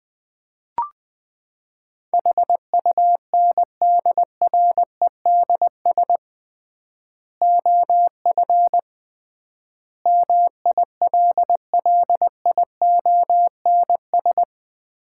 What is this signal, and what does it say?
Morse code, heard as a keyed tone, about 20 wpm, 700 Hz. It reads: HUNDREDS OF MILLIONS
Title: 3x Word-Spacing - Sets of 5 Words - 20wpm